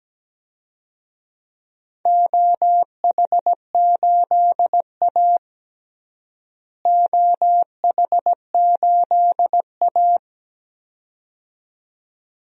Morse code, heard as a keyed tone, about 17 wpm, 700 Hz. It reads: OH8A OH8A
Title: DXCC-100 Call Signs - 4 Characters - Repeated 2x - 17wpm